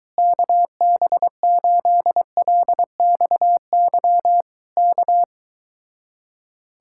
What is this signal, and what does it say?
Morse code, heard as a keyed tone, about 23 wpm, 700 Hz. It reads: KB8LXY K